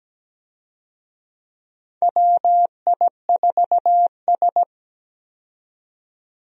Morse code, heard as a keyed tone, about 17 wpm, 700 Hz. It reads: WI4S
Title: USA Call Signs - 4 Characters - 17wpm